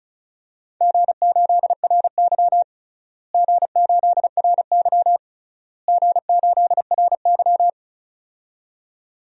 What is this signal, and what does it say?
Morse code, heard as a keyed tone, about 36 wpm, 700 Hz. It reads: G8RY G8RY G8RY
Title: DXCC-100 Call Signs - 4 Characters - Repeated 3x - 35wpm